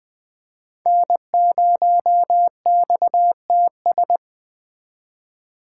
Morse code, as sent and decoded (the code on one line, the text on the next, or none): -. ----- -..- - ...
N0XTS